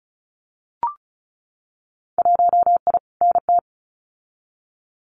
Morse code, heard as a keyed tone, about 36 wpm, 700 Hz. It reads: 1I NT
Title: Winter Field Day Exchanges Encore 35wpm